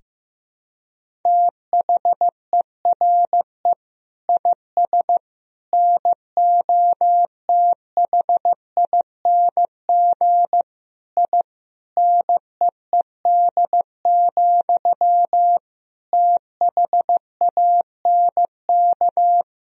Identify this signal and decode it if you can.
Morse code, heard as a keyed tone, about 15 wpm, 700 Hz. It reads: THERE IS NOTHING I NEED, THANK
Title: Sentences from Top 1000 Words - Review Part 1 - 15wpm